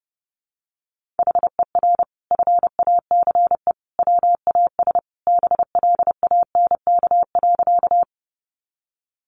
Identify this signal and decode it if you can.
Morse code, heard as a keyed tone, about 30 wpm, 700 Hz. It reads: HER FACE WAS BLANK.
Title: Sentences from Top 1500 Words 30wpm